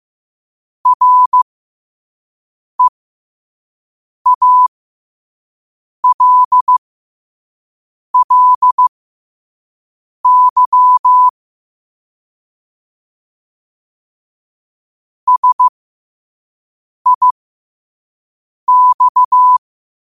Morse code, heard as a keyed tone, about 15 wpm, spaced as for 5 wpm, 1000 Hz. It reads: REALLY SIX
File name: qotdfiles / 5wpm